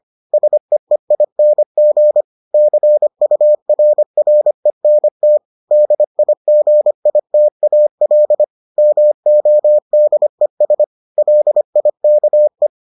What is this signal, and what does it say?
Morse code, heard as a keyed tone, about 25 wpm, 600 Hz. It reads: SEEING CURRENT DIGITAL MODES LIKE